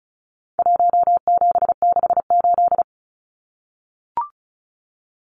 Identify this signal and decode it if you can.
Morse code, heard as a keyed tone, about 34 wpm, 700 Hz. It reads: 1768
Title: Numbers - Contest Style - 4-Digit 35wpm